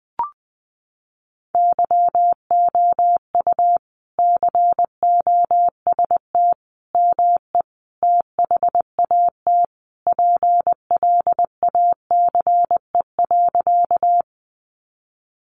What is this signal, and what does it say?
Morse code, heard as a keyed tone, about 20 wpm, 700 Hz. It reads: YOU COST ME THAT PLACE.
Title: Sentences from Top 800 Words 20wpm